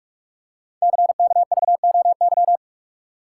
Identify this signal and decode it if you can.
Morse code, heard as a keyed tone, about 44 wpm, 690 Hz. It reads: CKVOY
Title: Code Groups - 5 Letters - 45wpm